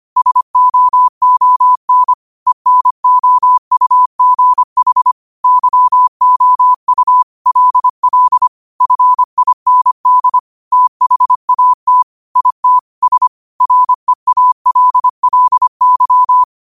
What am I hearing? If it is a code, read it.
Morse code, heard as a keyed tone, about 25 wpm, 1000 Hz. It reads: SOON ENOUGH YOU LL FIND THAT IT S REALLY
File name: qotdfiles / 25wpm